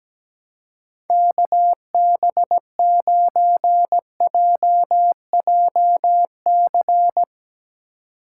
Morse code, heard as a keyed tone, about 17 wpm, 700 Hz. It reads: KB9JJC
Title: USA Call Signs - 6 Characters - 2x Character Spacing - 17wpm